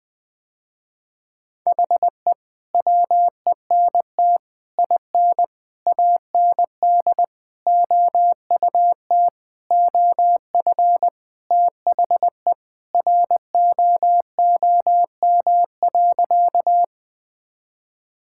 Morse code, heard as a keyed tone, about 20 wpm, 700 Hz. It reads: HE WENT IN AND OUT OF THE ROOM.
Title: Sentences with 5-Word Phrases - Spoken Beginning - 20wpm